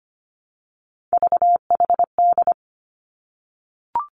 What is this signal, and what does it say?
Morse code, heard as a keyed tone, about 25 wpm, 700 Hz. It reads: VHD
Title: Three Challenging Letter-Number Combinations 25wpm